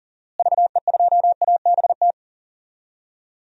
Morse code, heard as a keyed tone, about 40 wpm, 700 Hz. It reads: VE2ABT
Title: DXCC-100 Call Signs - 6 Characters - Repeated 3x - 40wpm